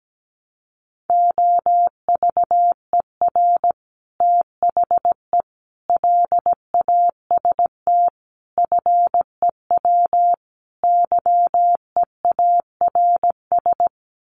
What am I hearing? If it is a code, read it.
Morse code, heard as a keyed tone, about 17 wpm, 700 Hz. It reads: OVER THE LAST FEW YEARS
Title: Sets of 5 Words Encore 17wpm